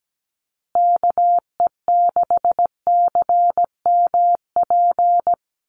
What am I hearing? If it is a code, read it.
Morse code, heard as a keyed tone, about 17 wpm, 700 Hz. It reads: KE6CMP
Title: USA Call Signs Repeated 3x 17wpm